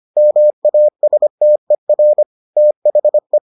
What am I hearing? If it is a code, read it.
Morse code, heard as a keyed tone, about 25 wpm, 600 Hz. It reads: MASTER THE